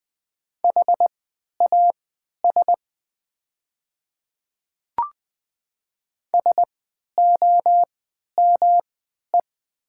Morse code, heard as a keyed tone, about 20 wpm, spaced as for 11 wpm, 700 Hz. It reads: HAS SOME
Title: 3x Character Spacing - Top 200 Words - 20wpm